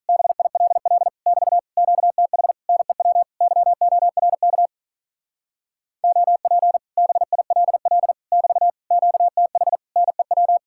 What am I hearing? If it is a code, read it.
Morse code, heard as a keyed tone, about 46 wpm, 700 Hz. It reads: BILL <BT> QTH NEW YORK OP BILL <BT> QTH NEW